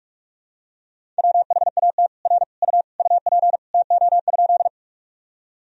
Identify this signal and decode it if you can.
Morse code, heard as a keyed tone, about 46 wpm, 700 Hz. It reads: WHAT R U UP TO?
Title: Sentences from Top 100 Words and Abbreviations 45wpm